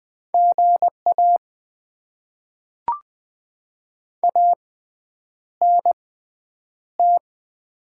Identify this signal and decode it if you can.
Morse code, heard as a keyed tone, about 20 wpm, spaced as for 9 wpm, 700 Hz. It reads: GA ANT
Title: Top 20 QSO Elements - 6x Character Spacing - 20wpm